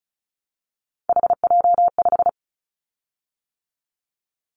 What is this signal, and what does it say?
Morse code, heard as a keyed tone, about 36 wpm, 700 Hz. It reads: HJ5
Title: Callsign Prefixes for Top 100 DXCC - 35wpm